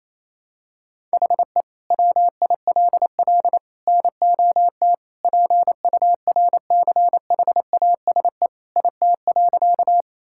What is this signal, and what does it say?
Morse code, heard as a keyed tone, about 28 wpm, 700 Hz. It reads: HE WILL NOT PURCHASE IT.
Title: Sentences from Top 1000 Words - Review Part 5 - 28wpm